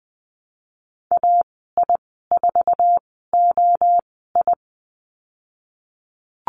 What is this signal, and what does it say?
Morse code, heard as a keyed tone, about 20 wpm, spaced as for 15 wpm, 700 Hz. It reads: AI4OI
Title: 055 Lesson 10g - Letter r - Call Signs 20wpm